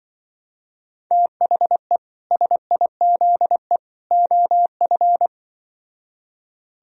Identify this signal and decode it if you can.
Morse code, heard as a keyed tone, about 23 wpm, 700 Hz. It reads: THE SIZE OF
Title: Speed-Racing - Sets of 3 Words 20wpm